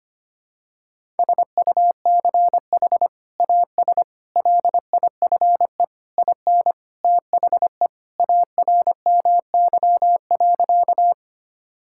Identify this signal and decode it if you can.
Morse code, heard as a keyed tone, about 25 wpm, 700 Hz. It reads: SUCH AS LIFE IN THE ARMY.
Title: Sentences from Top 1000 Words - Review Part 5 - 25wpm